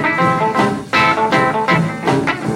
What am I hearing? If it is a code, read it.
There is no code.